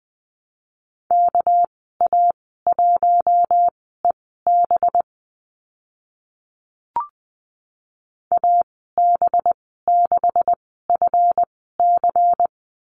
Morse code, heard as a keyed tone, about 20 wpm, spaced as for 15 wpm, 700 Hz. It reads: KA1EB AB6FC
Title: 213 Lesson 34g - Number 8 - Call Signs 20wpm